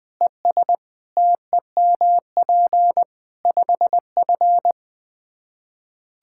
Morse code, heard as a keyed tone, about 21 wpm, 700 Hz. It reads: ES TEMP 5F
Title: Practice QSO - Exchange 2 - Info - 20wpm